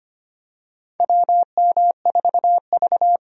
Morse code, heard as a keyed tone, about 25 wpm, 700 Hz. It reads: WM4V
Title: USA Call Signs - 4 Characters - 25wpm